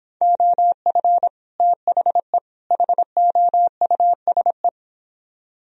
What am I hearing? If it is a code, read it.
Morse code, heard as a keyed tone, about 26 wpm, 700 Hz. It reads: OF THE HOUSE
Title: Speed-Racing - Sets of 3 Words 22wpm